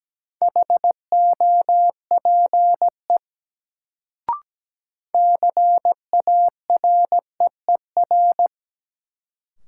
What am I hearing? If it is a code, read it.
Morse code, heard as a keyed tone, about 17 wpm, 700 Hz. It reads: HOPE CAREER
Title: Sentences from Top 1000 Words - New Word Drill - 17wpm